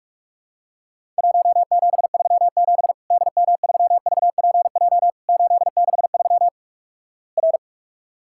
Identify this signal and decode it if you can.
Morse code, heard as a keyed tone, about 44 wpm, 700 Hz. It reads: 1737 DM3VPJ 863 R
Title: Worked All Europe DX Contest - QTC Exchange - 45wpm